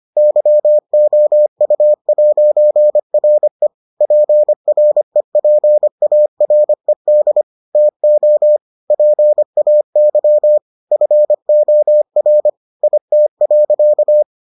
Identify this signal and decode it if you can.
Morse code, heard as a keyed tone, about 25 wpm, 600 Hz. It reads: YOU'RE PREPARED TO PAY FOR IT.